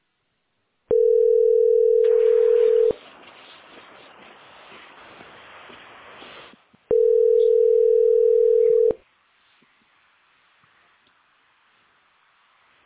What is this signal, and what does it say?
Ringback tone (North America), 440 + 480 Hz.